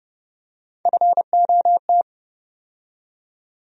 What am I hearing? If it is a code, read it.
Morse code, heard as a keyed tone, about 30 wpm, 700 Hz. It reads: FOT
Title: USA Call Sign Postfixes - 3 Letters 30wpm